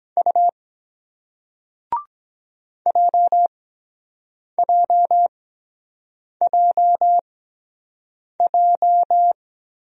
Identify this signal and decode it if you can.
Morse code, heard as a keyed tone, about 23 wpm, spaced as for 7 wpm, 700 Hz. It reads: U JJJJ